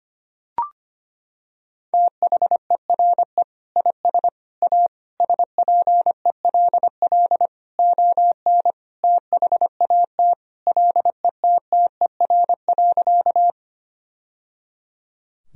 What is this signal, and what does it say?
Morse code, heard as a keyed tone, about 25 wpm, 700 Hz. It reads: THERE IS A SPELL ON THAT LETTER.
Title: Sentences from Top 1000 Words - Review Part 1 - 25wpm